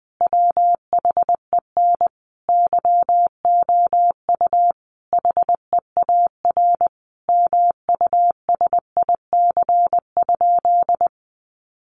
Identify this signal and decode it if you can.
Morse code, heard as a keyed tone, about 20 wpm, 700 Hz. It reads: WHEN YOU HEAR MUSIC?
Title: Sentences from Top 300 Words 20wpm